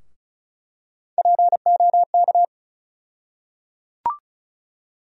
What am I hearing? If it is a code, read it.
Morse code, heard as a keyed tone, about 36 wpm, 700 Hz. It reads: POK
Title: USA Call Sign Postfixes - 3 Letters 35wpm